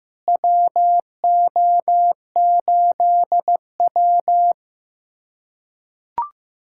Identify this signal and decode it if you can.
Morse code, heard as a keyed tone, about 15 wpm, 700 Hz. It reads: WO8W
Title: USA Callsign DE Callsign - 4 Characters - 15wpm